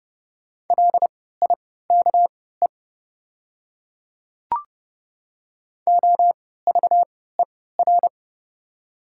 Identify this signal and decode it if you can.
Morse code, heard as a keyed tone, about 29 wpm, spaced as for 17 wpm, 700 Hz. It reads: LIKE OVER
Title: 3x Character Spacing - Top 100 Words - 30wpm